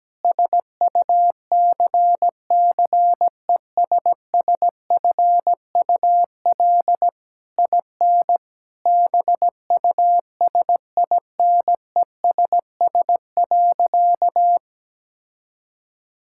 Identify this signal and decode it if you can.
Morse code, heard as a keyed tone, about 17 wpm, 700 Hz. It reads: SUCCESSFUL IN BUSINESS.
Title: Sentences from Top 1000 Words - Review Part 6 - 17wpm